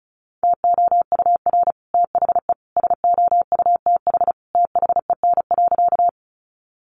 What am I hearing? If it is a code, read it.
Morse code, heard as a keyed tone, about 34 wpm, 700 Hz. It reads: TOUR THE SOUTH THEN.
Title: Sentences from Top 1400 Words 35wpm